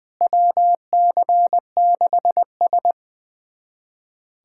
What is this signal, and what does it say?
Morse code, heard as a keyed tone, about 21 wpm, 700 Hz. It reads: WC6S
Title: USA Callsign DE Callsign - 4 Characters - 20wpm